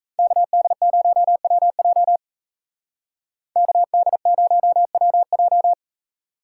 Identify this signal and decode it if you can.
Morse code, heard as a keyed tone, about 36 wpm, 700 Hz. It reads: KD0WJ KD0WJ